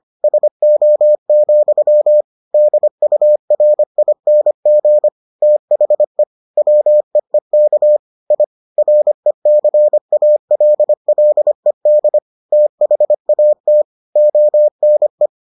Morse code, heard as a keyed tone, about 25 wpm, 600 Hz. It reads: SO, DURING THE WEEK I RECALLED THAT ONE